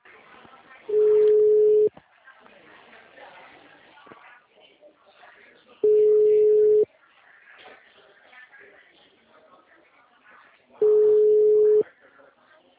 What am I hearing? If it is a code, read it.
Ringback tone (Europe, 425 Hz), with speech.
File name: out- -2001- - - .22134